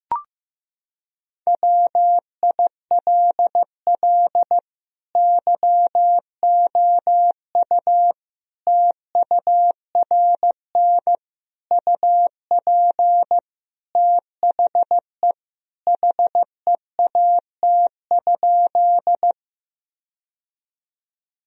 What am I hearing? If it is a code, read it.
Morse code, heard as a keyed tone, about 15 wpm, 700 Hz. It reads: WILL YOU TURN UP THE HEAT?
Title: Sentences from Top 1000 Words - Review Part 3 - 15wpm